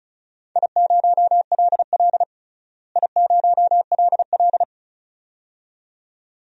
Morse code, heard as a keyed tone, about 36 wpm, 700 Hz. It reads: I0LL I0LL
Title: DXCC-100 Call Signs - 4 Characters - Repeated 2x - 35wpm